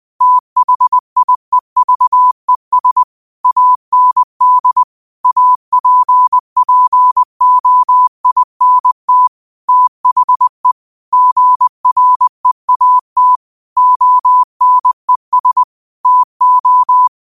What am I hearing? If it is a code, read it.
Morse code, heard as a keyed tone, about 20 wpm, 1000 Hz. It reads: THIEVES AND APPOINT THE GREAT ONES TO
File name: 20wpm